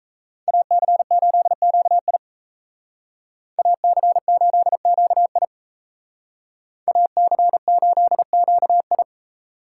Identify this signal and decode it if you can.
Morse code, heard as a keyed tone, about 35 wpm, 700 Hz. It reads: AC8QI AC8QI AC8QI